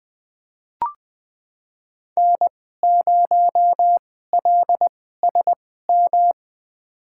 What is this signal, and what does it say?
Morse code, heard as a keyed tone, about 21 wpm, spaced as for 15 wpm, 700 Hz. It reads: N0LSM
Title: USA Call Signs - 5 Characters - 2x Character Spacing - 20wpm